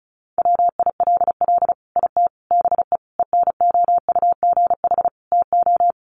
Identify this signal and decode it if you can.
Morse code, heard as a keyed tone, about 34 wpm, 700 Hz. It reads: WILL IT BE ENOUGH TO
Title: Sentences from Top 1100 Words 35wpm